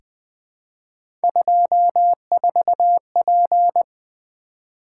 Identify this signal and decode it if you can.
Morse code, heard as a keyed tone, about 20 wpm, 700 Hz. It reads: 24P
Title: Three Letter-Number Combinations - 20wpm